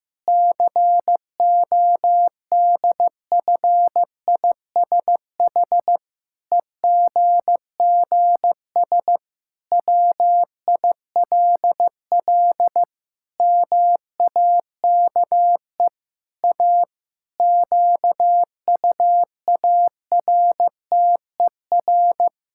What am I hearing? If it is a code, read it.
Morse code, heard as a keyed tone, about 15 wpm, 700 Hz. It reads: CODFISH EGGS WILL MAKE A QUARTER
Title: Pangram Challenge v2 - 15wpm